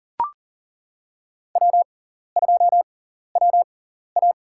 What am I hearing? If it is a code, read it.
Morse code, heard as a keyed tone, about 37 wpm, spaced as for 13 wpm, 700 Hz. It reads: W2WA
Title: USA Call Signs - 4 Characters - 6x Character Spacing - 40wpm